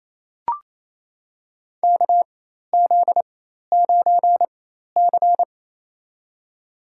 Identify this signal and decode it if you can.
Morse code, heard as a keyed tone, about 27 wpm, spaced as for 13 wpm, 700 Hz. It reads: KZ9C
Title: USA Call Signs - 4 Characters - 4x Character Spacing - 28wpm